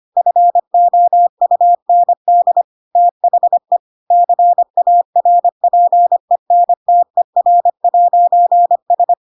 Morse code, heard as a keyed tone, about 25 wpm, 700 Hz. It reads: FOUND THE CARPENTER'S
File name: Pinocchio / Pinocchio ch2 25 25 ik1hge